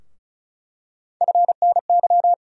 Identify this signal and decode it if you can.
Morse code, heard as a keyed tone, about 36 wpm, 700 Hz. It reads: FNY
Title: USA Call Sign Postfixes - 3 Letters 35wpm